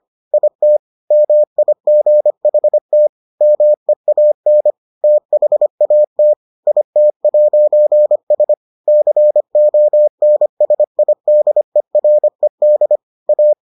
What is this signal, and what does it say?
Morse code, heard as a keyed tone, about 25 wpm, 600 Hz. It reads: IT MIGHT MEAN THAT IT'S CONSIDERED A